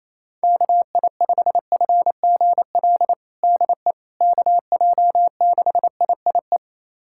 Morse code, heard as a keyed tone, about 28 wpm, 700 Hz. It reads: KI5FGL DE KJ6IIE
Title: USA Callsign DE Callsign - 6 Characters - 28wpm